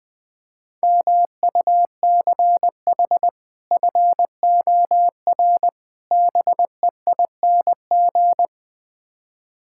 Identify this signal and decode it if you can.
Morse code, heard as a keyed tone, about 20 wpm, 700 Hz. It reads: MUCH FOR BEING
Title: Build A Phrase - 5-Word Phrases - 20wpm